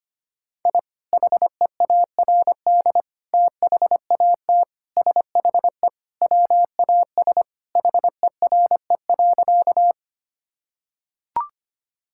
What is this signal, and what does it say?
Morse code, heard as a keyed tone, about 25 wpm, 700 Hz. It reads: I HEARD THAT SHE WAS HERE.
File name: practice / 25wpm / Sentences with 3-Word Phrases - 25wpm